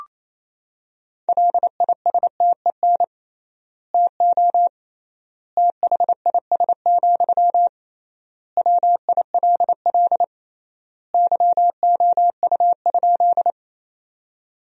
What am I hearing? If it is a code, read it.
Morse code, heard as a keyed tone, about 28 wpm, 700 Hz. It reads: LISTEN TO THIS, WILL YOU?
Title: 3x Word-Spacing - Sentences from Top 400 Words - 28wpm